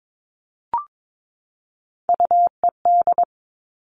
Morse code, heard as a keyed tone, about 22 wpm, 700 Hz. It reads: UED
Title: All Cut Numbers - 3-Digit 22wpm